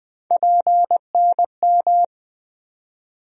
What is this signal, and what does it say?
Morse code, heard as a keyed tone, about 21 wpm, 700 Hz. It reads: PNM
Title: USA Call Sign Postfixes - 3 Letters 20wpm